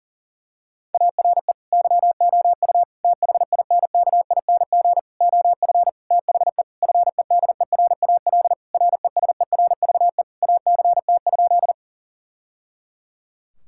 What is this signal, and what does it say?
Morse code, heard as a keyed tone, about 40 wpm, 700 Hz. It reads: ARE YOU THINKING OF THE FEDERAL RESERVE ACT?
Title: Sentences from Top 1000 Words - Review Part 6 - 40wpm